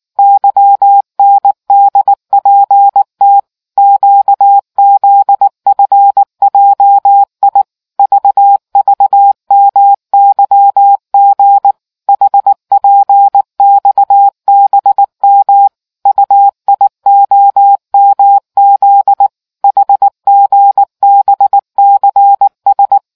Morse code, heard as a keyed tone, about 19 wpm, 800 Hz. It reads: YNDPT QZFJI VVMYG HPXBM UIOMZ HGBCS